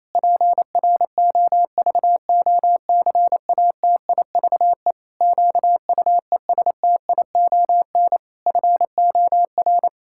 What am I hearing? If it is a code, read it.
Morse code, heard as a keyed tone, about 28 wpm, 700 Hz. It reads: PROVOCATIVE QUESTION FOR
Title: Pangram Challenge v2 - 28wpm